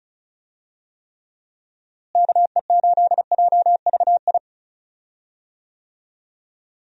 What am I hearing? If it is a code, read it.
Morse code, heard as a keyed tone, about 36 wpm, 700 Hz. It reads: KE8JVI
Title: USA Call Signs - 6 Characters - Repeated 2x - 35wpm